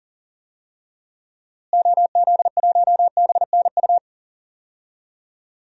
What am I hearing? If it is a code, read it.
Morse code, heard as a keyed tone, about 40 wpm, 700 Hz. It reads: OZ1BNU
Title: DXCC-100 Call Signs - 6 Characters - Repeated 2x - 40wpm